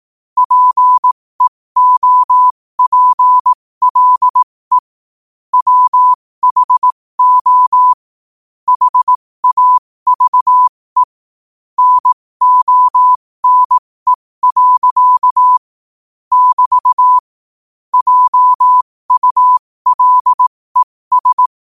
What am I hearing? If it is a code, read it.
Morse code, heard as a keyed tone, about 18 wpm, spaced as for 15 wpm, 1000 Hz. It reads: PEOPLE WHO HAVE NONE. <BT> JULES
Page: New quotes every day in morse code at 15 Words per minute.